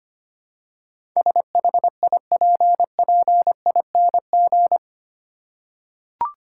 Morse code, heard as a keyed tone, about 25 wpm, 700 Hz. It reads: SHIPPING
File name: Eight Letter Words - 25wpm